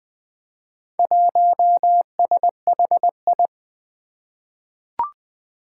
Morse code, heard as a keyed tone, about 20 wpm, 700 Hz. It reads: 1SHI
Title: Four Challenging Letter-Number Combinations 20wpm